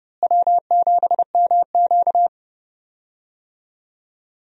Morse code, heard as a keyed tone, about 29 wpm, 700 Hz. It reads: W7MQ
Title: USA Call Signs - 4 Characters - Repeated 3x - 30wpm